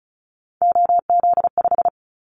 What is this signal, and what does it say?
Morse code, heard as a keyed tone, about 36 wpm, 690 Hz. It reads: OZ5
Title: Callsign Prefixes for Top 50 DXCC - 35wpm